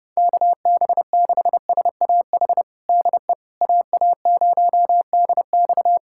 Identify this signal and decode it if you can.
Morse code, heard as a keyed tone, about 30 wpm, 700 Hz. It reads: KB6SAH DE AA0DX
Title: USA Callsign DE Callsign 30wpm